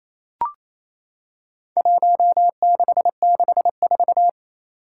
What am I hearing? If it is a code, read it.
Morse code, heard as a keyed tone, about 27 wpm, 700 Hz. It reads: 1664
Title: Four Challenging Letter-Number Combinations 28wpm